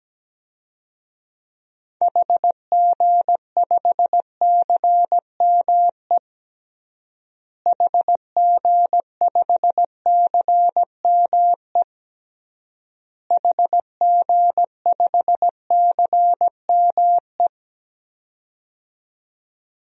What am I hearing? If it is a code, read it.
Morse code, heard as a keyed tone, about 17 wpm, 700 Hz. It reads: HG5CME HG5CME HG5CME
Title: DXCC-100 Call Signs - 6 Characters - Repeated 3x - 17wpm